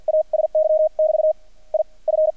Morse code, heard as a keyed tone, about 48 wpm, spaced as for 35 wpm, 620 Hz. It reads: ASY<BT> IF
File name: archived / CRSnet